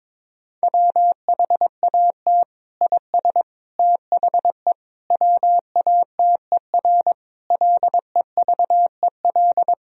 Morse code, heard as a keyed tone, about 22 wpm, 700 Hz. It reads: WHAT IS THE WATER LEVEL
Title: Sentences from Top 1000 Words - Review Part 4 - 22wpm